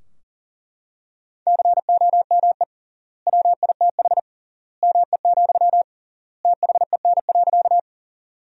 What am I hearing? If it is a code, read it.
Morse code, heard as a keyed tone, about 40 wpm, 700 Hz. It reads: COME WITH ME, THEN.
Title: 3x Word-Spacing - Sentences from Top 200 Words - 40wpm